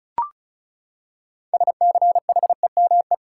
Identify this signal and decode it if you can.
Morse code, heard as a keyed tone, about 34 wpm, 700 Hz. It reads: SCHEME
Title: Six Letter Words - 35wpm